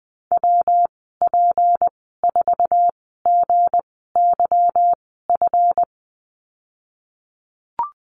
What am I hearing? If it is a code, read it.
Morse code, heard as a keyed tone, about 20 wpm, spaced as for 15 wpm, 700 Hz. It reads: WP4GYF
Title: 179 Lesson 29g - Slash - Call Signs 20wpm